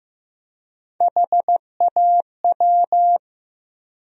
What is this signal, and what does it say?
Morse code, heard as a keyed tone, about 15 wpm, 700 Hz. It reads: HAW
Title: USA Call Sign Postfixes - 3 Letters 15wpm